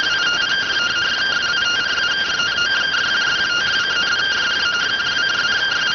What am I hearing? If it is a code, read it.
Two-tone teleprinter signal, about 170 Hz shift.